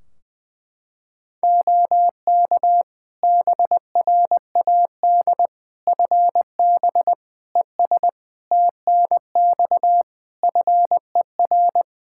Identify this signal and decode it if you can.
Morse code, heard as a keyed tone, about 20 wpm, 700 Hz. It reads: OK BRAD FB ES TNX FER
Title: K1USN SST QSO Practice 20wpm